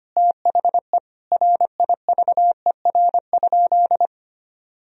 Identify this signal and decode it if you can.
Morse code, heard as a keyed tone, about 25 wpm, 700 Hz. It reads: THE RIVER?
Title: Sentences from Top 1000 Words - Review Part 4 - 25wpm